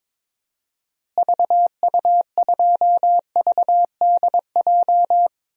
Morse code, heard as a keyed tone, about 22 wpm, 700 Hz. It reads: VU2VDJ
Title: DXCC-100 Call Signs - 6 Characters - Repeated 2x - 22wpm